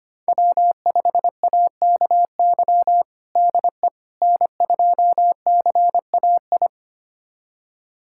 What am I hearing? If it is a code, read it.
Morse code, heard as a keyed tone, about 25 wpm, 700 Hz. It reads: W5AKY DE N2CAI
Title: USA Callsign DE Callsign - 5 Characters - 25wpm